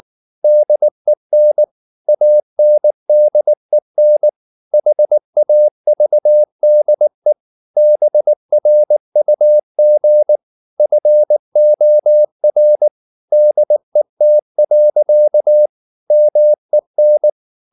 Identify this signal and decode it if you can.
Morse code, heard as a keyed tone, about 19 wpm, 600 Hz. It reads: DEN ANDEN HAVDE BRUG FOR DET. MEN
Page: Ferie 19 wpm